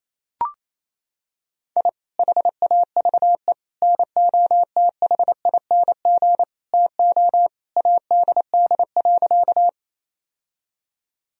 Morse code, heard as a keyed tone, about 28 wpm, 700 Hz. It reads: I HAVE NOTHING TO ADD.
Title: Sentences from Top 1000 Words - Review Part 1 - 28wpm